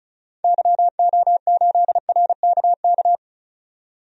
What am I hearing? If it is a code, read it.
Morse code, heard as a keyed tone, about 36 wpm, 700 Hz. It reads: YO8RKK
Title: DXCC-100 Call Signs - 6 Characters - 35wpm